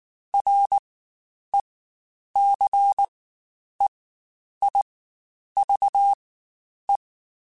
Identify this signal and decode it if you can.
Morse code, heard as a keyed tone, about 19 wpm, spaced as for 9 wpm, 800 Hz. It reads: RECEIVE
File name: Quotes Spaced / Quotes Spaced-mp3s / Advice sp @18